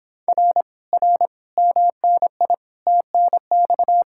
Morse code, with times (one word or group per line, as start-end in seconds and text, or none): 0.3-0.6 R
0.9-1.3 R
1.6-2.5 MNI
2.9-4.0 TNX